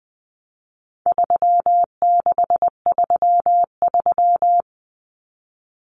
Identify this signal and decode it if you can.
Morse code, heard as a keyed tone, about 20 wpm, 700 Hz. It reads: 3633
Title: Common Cut Numbers - 4-Digit 20wpm